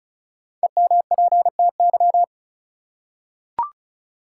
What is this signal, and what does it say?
Morse code, heard as a keyed tone, about 36 wpm, 700 Hz. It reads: EMPTY
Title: Five Letter Words - 35wpm